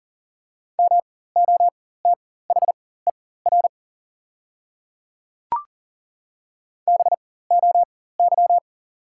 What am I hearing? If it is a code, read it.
Morse code, heard as a keyed tone, about 40 wpm, spaced as for 19 wpm, 690 Hz. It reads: MOTHER BOY
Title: 4x Character Spacing - Top 500 Words - 40wpm